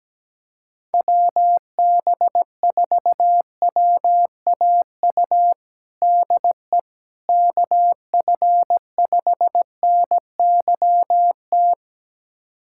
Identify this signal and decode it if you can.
Morse code, heard as a keyed tone, about 17 wpm, 700 Hz. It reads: WB4WAU DE KF5NYT
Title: USA Callsign DE Callsign - 6 Characters - 17wpm